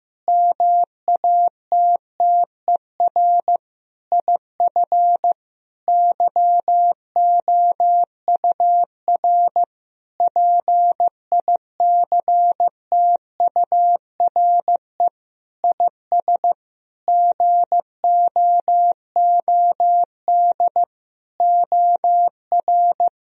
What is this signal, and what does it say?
Morse code, heard as a keyed tone, about 16 wpm, 700 Hz. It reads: MATTER IF YOUR PICTURE IS GOOD OR
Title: Sentences from Top 1000 Words - Review Part 1 - 15wpm